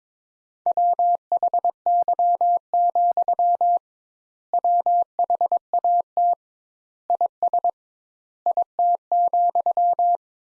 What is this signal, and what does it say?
Morse code, heard as a keyed tone, about 22 wpm, 700 Hz. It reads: WHY, WHAT IS IT,
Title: Sentences Top 200 Words - 22wpm